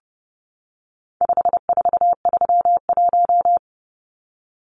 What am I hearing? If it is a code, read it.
Morse code, heard as a keyed tone, about 30 wpm, 700 Hz. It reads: 5431
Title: Numbers - 4-Digit 30wpm